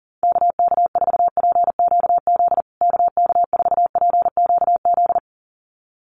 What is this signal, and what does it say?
Morse code, heard as a keyed tone, about 40 wpm, 700 Hz. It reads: KK4PQZ KK4PQZ
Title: USA Call Signs Repeated 2x 40wpm